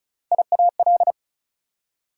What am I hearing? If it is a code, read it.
Morse code, heard as a keyed tone, about 36 wpm, 700 Hz. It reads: IAL
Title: Prefixes and Suffixes 35wpm